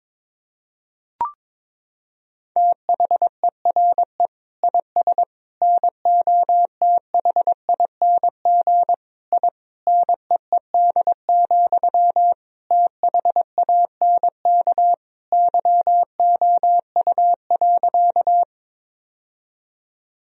Morse code, heard as a keyed tone, about 22 wpm, 700 Hz. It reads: THERE IS NOTHING I NEED, THANK YOU.
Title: Sentences from Top 1000 Words - Review Part 1 - 22wpm